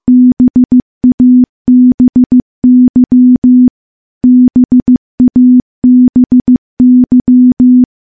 Morse code, heard as a keyed tone, about 15 wpm, 250 Hz. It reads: BABY BABY
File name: morse-training-15WPM